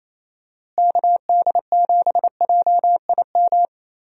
Morse code, heard as a keyed tone, about 28 wpm, 700 Hz. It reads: KD7JIM
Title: USA Call Signs - 6 Characters - Repeated 3x - 28wpm